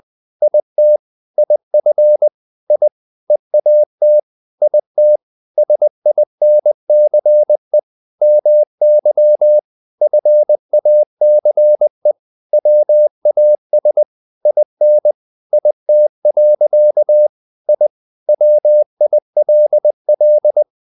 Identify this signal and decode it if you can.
Morse code, heard as a keyed tone, about 20 wpm, 600 Hz. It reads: IT IF I EAT IT SINCE MY FACE WAS IN IT. I WILL
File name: the old man and the sea part 2 20wpm